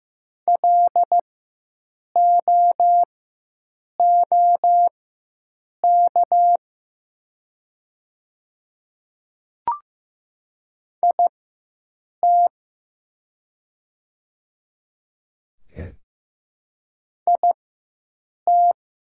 Morse code, heard as a keyed tone, about 15 wpm, spaced as for 7 wpm, 700 Hz. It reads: LOOK IT IT
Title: 4x Character Spacing - Top 200 Words - 15wpm